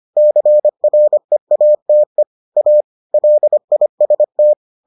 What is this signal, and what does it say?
Morse code, heard as a keyed tone, about 25 wpm, 600 Hz. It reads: CREATE A LIST